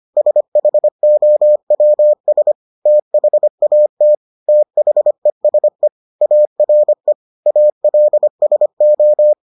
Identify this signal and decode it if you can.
Morse code, heard as a keyed tone, about 25 wpm, 600 Hz. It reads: SHOWS THAT THESE ARE ALSO